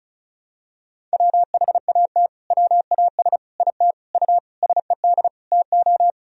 Morse code, heard as a keyed tone, about 34 wpm, 690 Hz. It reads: WHAT WAS IT U SED TO
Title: Sentences from Top 200 Words and Abbreviations 35wpm